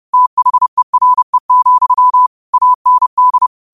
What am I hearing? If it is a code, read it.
Morse code, heard as a keyed tone, about 30 wpm, 1000 Hz. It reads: THERE, AND
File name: qotdfiles / 30wpm